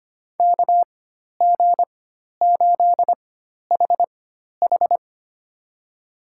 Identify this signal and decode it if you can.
Morse code, heard as a keyed tone, about 25 wpm, spaced as for 12 wpm, 700 Hz. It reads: KG8HH
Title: USA Call Signs - 5 Characters - 4x Character Spacing - 25wpm